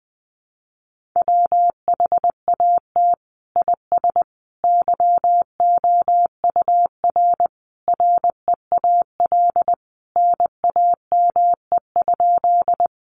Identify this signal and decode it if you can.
Morse code, heard as a keyed tone, about 20 wpm, 700 Hz. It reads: WHAT IS YOUR REAL NAME?
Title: Sentences from Top 300 Words 20wpm